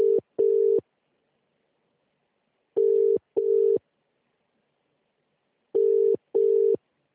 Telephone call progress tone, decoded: ringback tone